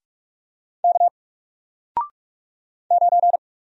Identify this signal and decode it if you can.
Morse code, heard as a keyed tone, about 48 wpm, 700 Hz. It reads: K 9
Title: Single Letter-Number Alt 45wpm